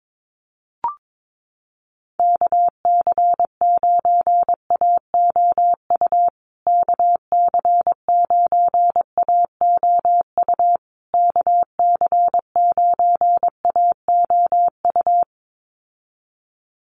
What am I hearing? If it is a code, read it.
Morse code, heard as a keyed tone, about 22 wpm, 700 Hz. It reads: KC9AOU KC9AOU KC9AOU
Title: USA Call Signs Repeated 3x 22wpm